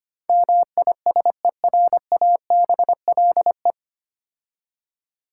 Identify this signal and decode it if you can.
Morse code, heard as a keyed tone, about 25 wpm, 700 Hz. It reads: MISERABLE
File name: practice / 25wpm / 9 Letter Words - First 3 Letters Given - 25wpm